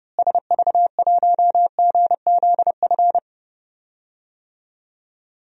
Morse code, heard as a keyed tone, about 30 wpm, 700 Hz. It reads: SV1GZF
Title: DXCC-100 Call Signs - 6 Characters - Repeated 3x - 30wpm